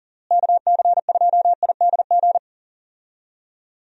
Morse code, heard as a keyed tone, about 37 wpm, 700 Hz. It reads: KC2IDG
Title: USA Call Signs - 6 Characters - Repeated 2x - 40wpm